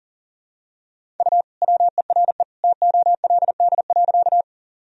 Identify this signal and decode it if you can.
Morse code, heard as a keyed tone, about 37 wpm, 700 Hz. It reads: U WERE TOLD.
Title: Sentences from Top 400 Words and Abbreviations 40wpm